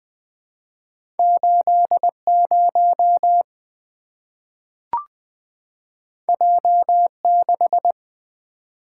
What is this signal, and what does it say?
Morse code, heard as a keyed tone, about 20 wpm, 700 Hz. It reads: 80 J6
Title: Two Letter-Number Combinations - 20wpm